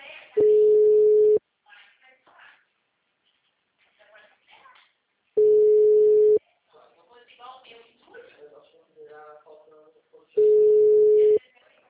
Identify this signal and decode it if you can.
Ringback tone (Europe, 425 Hz), with speech.
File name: out- -2004- - - .10